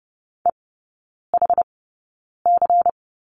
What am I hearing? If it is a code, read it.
Morse code, heard as a keyed tone, about 30 wpm, spaced as for 9 wpm, 690 Hz. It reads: EHC
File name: Lesson 16e - Letter c - Three Characters 30wpm